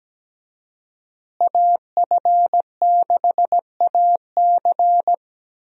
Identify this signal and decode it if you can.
Morse code, heard as a keyed tone, about 17 wpm, 700 Hz. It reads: AF6AC
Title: USA Call Signs - 5 Characters - Repeated 3x - 17wpm